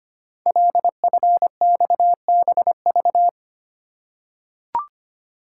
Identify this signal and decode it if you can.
Morse code, heard as a keyed tone, about 25 wpm, 700 Hz. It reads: LFXBV
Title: Code Groups - 5 Letters - 25wpm